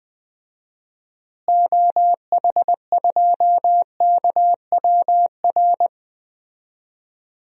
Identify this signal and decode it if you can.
Morse code, heard as a keyed tone, about 21 wpm, 700 Hz. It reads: OH2KWR
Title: DXCC-100 Call Signs - 6 Characters - Repeated 2x - 20wpm